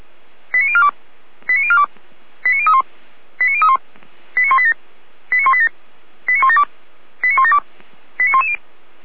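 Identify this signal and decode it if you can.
Five-tone selective call (ZVEI), heard as quick runs of five tones, tones of about 70 ms. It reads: 89042 89042 89031 89031 89187 89187 89172 89172 89110
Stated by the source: ZVEI five-tone call 89160 received via SDR, on the Feuerwehr Orlamuende audio server.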